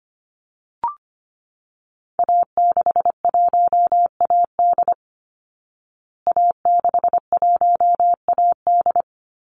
Morse code, heard as a keyed tone, about 25 wpm, 700 Hz. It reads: A61AD A61AD
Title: Call Signs Repeated 2x - 25wpm